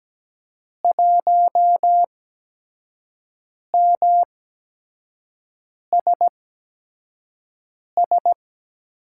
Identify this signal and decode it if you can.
Morse code, heard as a keyed tone, about 17 wpm, spaced as for 5 wpm, 700 Hz. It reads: 1MSS
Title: USA Call Signs - 8x Character Spacing - 17wpm